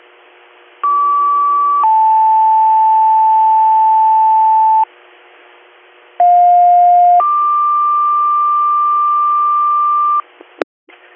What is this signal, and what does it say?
Two-tone page, heard as a one-second tone, then a three-second tone, about 1150 then 880 Hz, about 710 then 1180 Hz. It is heard from 0.8 to 10.2 s.